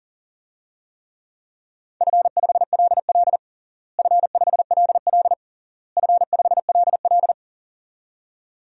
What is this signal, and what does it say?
Morse code, heard as a keyed tone, about 37 wpm, 700 Hz. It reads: F5LL F5LL F5LL
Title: DXCC-100 Call Signs - 4 Characters - Repeated 3x - 40wpm